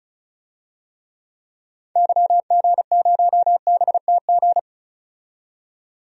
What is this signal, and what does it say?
Morse code, heard as a keyed tone, about 36 wpm, 700 Hz. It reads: YG0BTG
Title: DXCC-100 Call Signs - 6 Characters - 35wpm